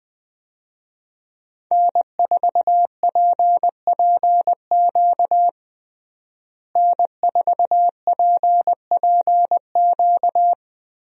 Morse code, heard as a keyed tone, about 20 wpm, 700 Hz. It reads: N4PPQ N4PPQ
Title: USA Call Signs - 5 Characters - Repeated 3x - 20wpm